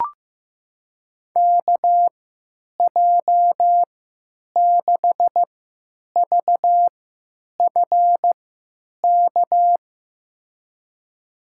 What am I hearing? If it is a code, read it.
Morse code, heard as a keyed tone, about 15 wpm, spaced as for 9 wpm, 700 Hz. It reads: KJ6VFK
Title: USA Call Signs - 6 Characters - 3x Character Spacing - 15wpm